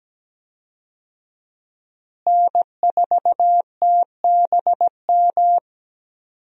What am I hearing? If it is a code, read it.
Morse code, heard as a keyed tone, about 17 wpm, 700 Hz. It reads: N4TBM
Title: USA Call Signs - 5 Characters - Repeated 3x - 17wpm